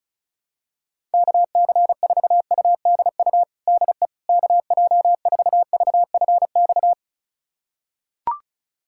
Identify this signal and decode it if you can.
Morse code, heard as a keyed tone, about 34 wpm, 700 Hz. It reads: KC4UDU DE KJ4VFX
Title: USA Callsign DE Callsign - 6 Characters - 35wpm